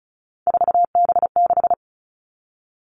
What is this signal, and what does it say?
Morse code, heard as a keyed tone, about 34 wpm, 700 Hz. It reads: 4B6
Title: All Cut Numbers - 3-Digit 35wpm